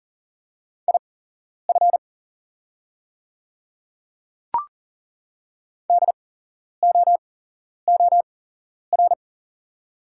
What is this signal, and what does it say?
Morse code, heard as a keyed tone, about 40 wpm, spaced as for 11 wpm, 690 Hz. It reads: IF DOOR
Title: 8x Character Spacing - Top 500 Words - 40wpm